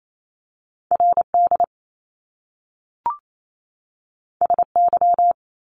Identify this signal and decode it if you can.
Morse code, heard as a keyed tone, about 28 wpm, 700 Hz. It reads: R D S Y
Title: Top Two Letter Combinations in Words 28wpm